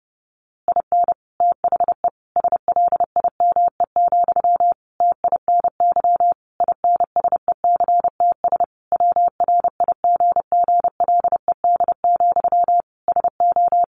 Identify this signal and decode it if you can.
Morse code, heard as a keyed tone, about 30 wpm, 700 Hz. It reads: IN THE SLIME, TINY INSECTS WRIGGLED, SO
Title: A Sound Of Thunder 30wpm